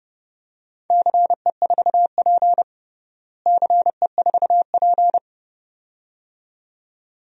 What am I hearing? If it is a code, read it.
Morse code, heard as a keyed tone, about 32 wpm, 700 Hz. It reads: CE4P CE4P
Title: DXCC-100 Call Signs - 4 Characters - Repeated 2x - 30wpm